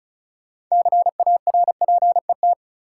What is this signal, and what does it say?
Morse code, heard as a keyed tone, about 36 wpm, 700 Hz. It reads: CARPET
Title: Six Letter Words - 35wpm